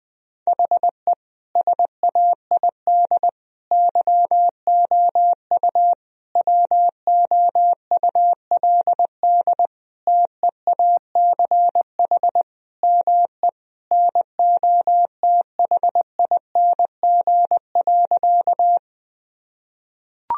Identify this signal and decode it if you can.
Morse code, heard as a keyed tone, about 20 wpm, 700 Hz. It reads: HE SAID YOU WOULD TEACH ME NOTHING.
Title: Sentences from Top 1000 Words - Review Part 2 - 20wpm